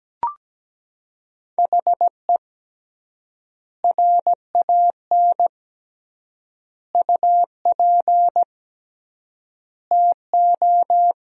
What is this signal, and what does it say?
Morse code, heard as a keyed tone, about 17 wpm, 700 Hz. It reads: HE RAN UP TO
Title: 3x Word-Spacing - Sentences from Top 500 Words - 17wpm